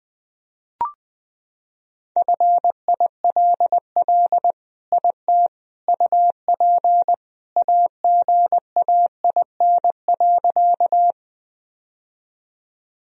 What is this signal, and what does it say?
Morse code, heard as a keyed tone, about 20 wpm, 700 Hz. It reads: FILL IT UP AGAIN.
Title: Sentences from Top 1000 Words - Review Part 3 - 20wpm